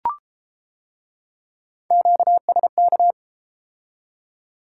Morse code, heard as a keyed tone, about 31 wpm, 700 Hz. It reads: QSK